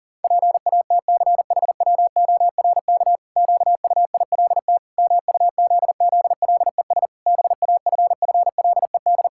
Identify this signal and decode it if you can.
Morse code, heard as a keyed tone, about 40 wpm, 700 Hz. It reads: PATCHWORK QUILT MUZZLES BAFFLED
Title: Pangram Challenge v2 - 40wpm